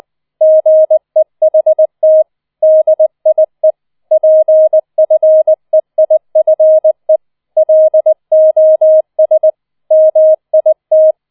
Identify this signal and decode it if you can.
Morse code, heard as a keyed tone, about 20 wpm, 630 Hz. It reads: GEHT DIE PFEIFE LOS MIT